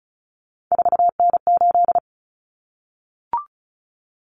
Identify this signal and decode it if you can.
Morse code, heard as a keyed tone, about 34 wpm, 700 Hz. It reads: 4N8
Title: Common Cut Numbers - 3-Digit 35wpm